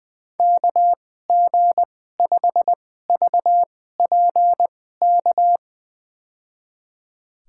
Morse code, heard as a keyed tone, about 20 wpm, spaced as for 14 wpm, 700 Hz. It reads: KG5VPK
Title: USA Call Signs - 2x Character Spacing - 20wpm